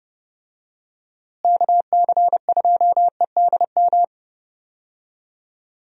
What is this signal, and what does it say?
Morse code, heard as a keyed tone, about 30 wpm, 700 Hz. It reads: KC2EDM
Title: USA Call Signs - 6 Characters - Repeated 2x - 30wpm